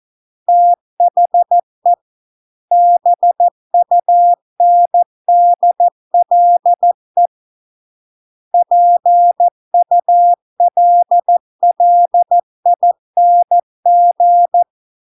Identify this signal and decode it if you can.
Morse code, heard as a keyed tone, about 14 wpm, 700 Hz. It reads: THE BUNDLE PULLING